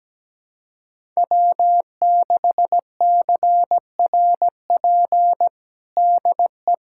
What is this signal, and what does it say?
Morse code, heard as a keyed tone, about 17 wpm, 700 Hz. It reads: W6CRP DE
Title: USA Callsign DE Callsign - 5 Characters - 17wpm